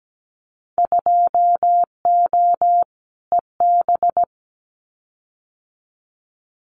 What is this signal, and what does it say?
Morse code, heard as a keyed tone, about 17 wpm, 700 Hz. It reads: 2O EB
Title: Winter Field Day Exchanges Encore 17wpm